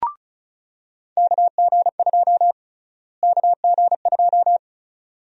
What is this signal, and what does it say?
Morse code, heard as a keyed tone, about 36 wpm, 700 Hz. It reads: KG2 KG2
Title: USA Call Sign Prefixes - Repeated 2x - 35wpm